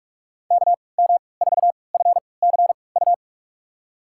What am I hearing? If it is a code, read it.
Morse code, heard as a keyed tone, about 47 wpm, spaced as for 26 wpm, 700 Hz. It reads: KM4FCU
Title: USA Call Signs - 6 Characters - 3x Character Spacing - 45wpm